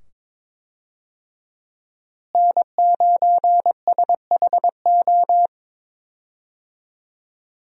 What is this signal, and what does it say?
Morse code, heard as a keyed tone, about 21 wpm, 700 Hz. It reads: N9SHO
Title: USA Call Signs - 5 Characters - Repeated 2x - 22wpm